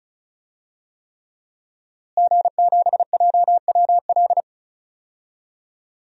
Morse code, heard as a keyed tone, about 36 wpm, 700 Hz. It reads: G7JWL
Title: DXCC-100 Call Signs - 5 Characters - Repeated 3x - 35wpm